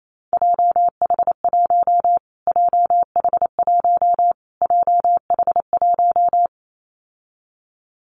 Morse code, heard as a keyed tone, about 28 wpm, 700 Hz. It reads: JH1 JH1 JH1
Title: Callsign Prefixes for Top 25 DXCC - Repeated 3x - 28wpm